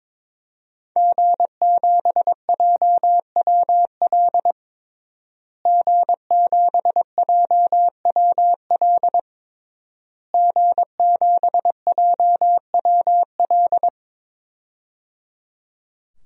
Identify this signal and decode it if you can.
Morse code, heard as a keyed tone, about 23 wpm, 700 Hz. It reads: G7JWL G7JWL G7JWL
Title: DXCC-100 Call Signs - 5 Characters - Repeated 3x - 22wpm